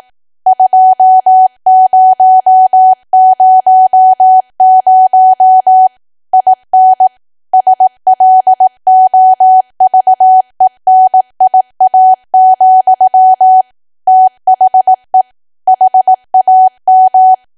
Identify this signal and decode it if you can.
Morse code, heard as a keyed tone, about 18 wpm, 750 Hz. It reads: 2000 IN SLOVENIA, THE HAM
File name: Archive / 18 WPM / 18WPM